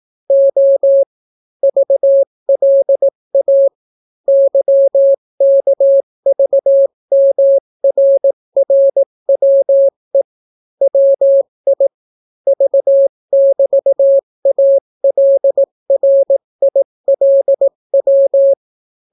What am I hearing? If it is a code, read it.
Morse code, heard as a keyed tone, about 18 wpm, 550 Hz. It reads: O VLA YKVMRRWE WI V<BT>ALRILW